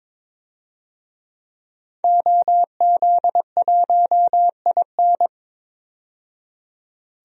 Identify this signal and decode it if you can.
Morse code, heard as a keyed tone, about 22 wpm, 700 Hz. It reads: OZ1IN
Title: DXCC-100 Call Signs - 5 Characters - 22wpm